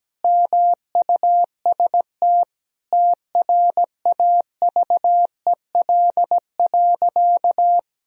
Morse code, heard as a keyed tone, about 17 wpm, 700 Hz. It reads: MUST TRAVEL.
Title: Sentences from Top 1000 Words - Review Part 2 - 17wpm